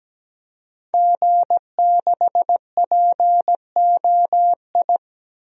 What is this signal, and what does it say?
Morse code, heard as a keyed tone, about 17 wpm, 700 Hz. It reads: G6POI
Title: DXCC-100 Call Signs - 5 Characters - 17wpm